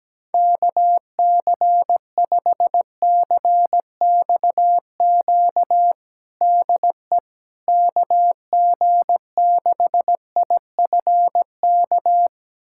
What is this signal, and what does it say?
Morse code, heard as a keyed tone, about 17 wpm, 700 Hz. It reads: KC5CXQ DE KG6IFK
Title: USA Callsign DE Callsign 17wpm